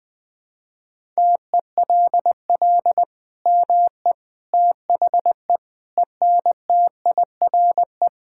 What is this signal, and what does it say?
Morse code, heard as a keyed tone, about 20 wpm, 700 Hz. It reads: TELL ME THE ENTIRE
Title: Sentences from Top 1000 Words - Review Part 5 - 20wpm